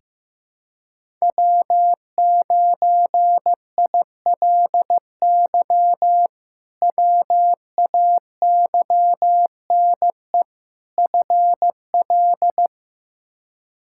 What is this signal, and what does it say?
Morse code, heard as a keyed tone, about 16 wpm, 700 Hz. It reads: W9ILY WAYNE FL
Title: K1USN SST QSO Practice - 15wpm